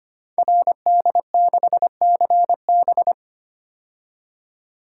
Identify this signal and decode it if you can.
Morse code, heard as a keyed tone, about 25 wpm, 700 Hz. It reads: RD6CB
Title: Code Groups - 5 Letter:Numbers - 25wpm